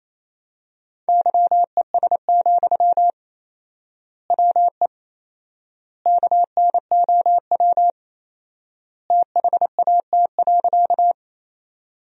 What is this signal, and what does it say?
Morse code, heard as a keyed tone, about 28 wpm, 700 Hz. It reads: YES, WE KNOW THAT.
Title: 4x Word-Spacing - Sentences from Top 500 Words - 28wpm